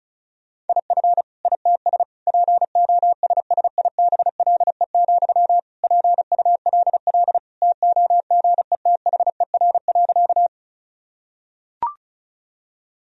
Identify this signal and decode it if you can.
Morse code, heard as a keyed tone, about 34 wpm, 700 Hz. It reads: IF ITS POSSIBLE, PULL TOGETHER.
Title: Sentences from Top 1000 Words - Review Part 2 - 35wpm